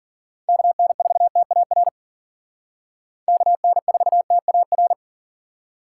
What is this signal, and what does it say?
Morse code, heard as a keyed tone, about 51 wpm, 700 Hz. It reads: KN4TAR KN4TAR